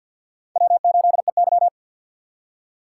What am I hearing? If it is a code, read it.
Morse code, heard as a keyed tone, about 49 wpm, 700 Hz. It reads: W8EY
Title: USA Call Signs - 4x Character Spacing - 50wpm